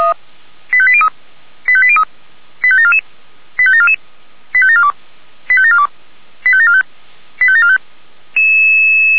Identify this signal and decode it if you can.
Five-tone selective call (ZVEI), heard as quick runs of five tones, tones of about 70 ms. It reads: 85803 85803 85744 85744 85742 85742 85745 85745